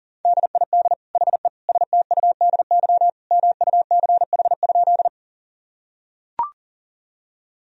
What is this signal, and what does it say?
Morse code, heard as a keyed tone, about 40 wpm, 700 Hz. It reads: DID HE STUDY MUCH?
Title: Sentences from Top 1000 Words - Review Part 1 - 40wpm